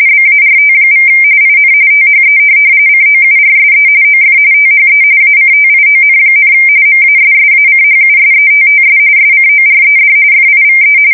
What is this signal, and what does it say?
RTTY, about 45.45 baud. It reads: BART= IN THE COURSE OF THE ACTION, MASSACHUSETTS SCORED FIVE